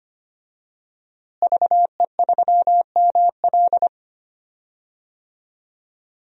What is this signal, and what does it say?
Morse code, heard as a keyed tone, about 25 wpm, 700 Hz. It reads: VE3ML
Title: DXCC-100 Call Signs - 5 Characters - Repeated 3x - 25wpm